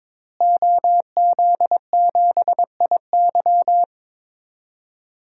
Morse code, heard as a keyed tone, about 22 wpm, 700 Hz. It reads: OZ7IY
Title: DXCC-100 Call Signs - 5 Characters - Repeated 3x - 22wpm